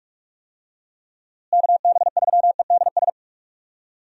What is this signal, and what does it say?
Morse code, heard as a keyed tone, about 46 wpm, 700 Hz. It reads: KB3EDS
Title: USA Call Signs - 6 Characters - Repeated 3x - 45wpm